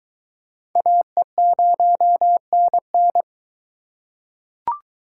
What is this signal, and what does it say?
Morse code, heard as a keyed tone, about 23 wpm, 700 Hz. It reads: AE0NN